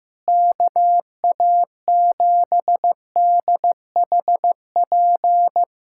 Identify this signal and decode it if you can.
Morse code, heard as a keyed tone, about 15 wpm, 700 Hz. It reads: KA7DHP
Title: USA Callsign DE Callsign 15wpm